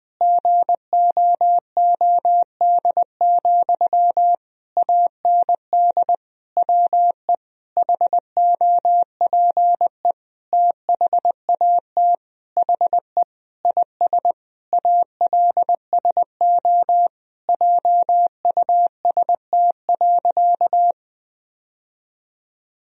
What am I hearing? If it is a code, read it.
Morse code, heard as a keyed tone, about 21 wpm, 700 Hz. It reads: GOOD, AND WE HOPE THAT HE IS ALSO JUST.
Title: Sentences from Top 1000 Words - Review Part 1 - 20wpm